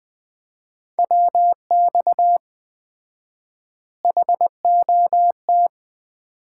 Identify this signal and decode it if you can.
Morse code, heard as a keyed tone, about 20 wpm, 700 Hz. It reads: WX HOT
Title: Common QSO Phrases - 4x Word-Spacing - 20wpm